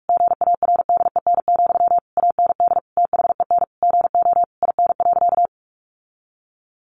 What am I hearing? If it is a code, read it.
Morse code, heard as a keyed tone, about 44 wpm, 700 Hz. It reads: GARDEN, AND THEN GO IN.
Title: Sentences from Top 700 Words 45wpm